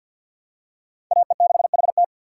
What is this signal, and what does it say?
Morse code, heard as a keyed tone, about 46 wpm, 700 Hz. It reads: AE6HT
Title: USA Call Signs - 5 Characters - 4x Character Spacing - 50wpm